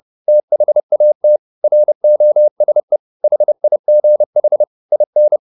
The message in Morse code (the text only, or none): THAT ROSE HIGH IN